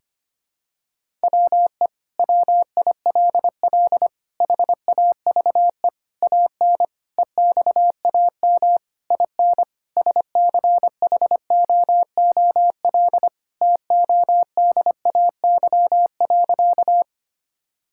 Morse code, heard as a keyed tone, about 25 wpm, 700 Hz. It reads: WE WILL HAVE AN EXAM IN SCHOOL TODAY.
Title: Sentences from Top 1000 Words - Review Part 5 - 25wpm